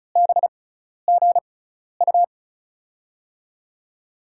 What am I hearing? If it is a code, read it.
Morse code, heard as a keyed tone, about 36 wpm, spaced as for 12 wpm, 700 Hz. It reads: BGU
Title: USA Call Signs - 6x Character Spacing - 35wpm